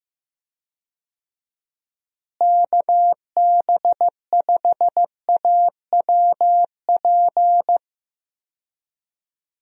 Morse code, heard as a keyed tone, about 15 wpm, 700 Hz. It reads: KB5AWP
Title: USA Call Signs - 6 Characters - Repeated 2x - 15wpm